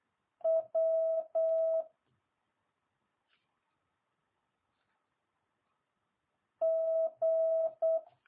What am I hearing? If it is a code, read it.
Morse code, heard as a keyed tone, about 8 wpm, 650 Hz. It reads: W G